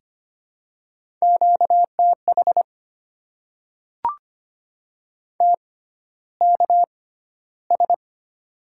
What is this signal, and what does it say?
Morse code, heard as a keyed tone, about 25 wpm, spaced as for 13 wpm, 700 Hz. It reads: QTH TKS
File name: Top 90 QSO Elements - 6x Character Spacing - 25wpm